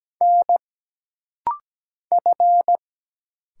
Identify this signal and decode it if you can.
Morse code, heard as a keyed tone, about 17 wpm, 700 Hz. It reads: N F
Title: Single Letters - Rapid-Fire 17wpm